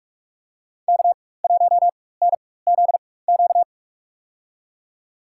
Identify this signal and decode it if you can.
Morse code, heard as a keyed tone, about 43 wpm, spaced as for 21 wpm, 700 Hz. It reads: K1NZQ
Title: USA Call Signs - 5 Characters - 4x Character Spacing - 45wpm